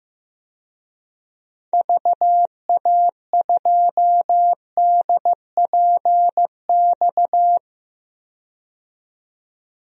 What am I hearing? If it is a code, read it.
Morse code, heard as a keyed tone, about 15 wpm, 700 Hz. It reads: VA2DPX
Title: DXCC-100 Call Signs - 6 Characters - 15wpm